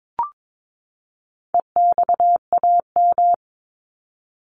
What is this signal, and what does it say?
Morse code, heard as a keyed tone, about 22 wpm, 700 Hz. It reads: EXAM
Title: Top 501-750 Words 22wpm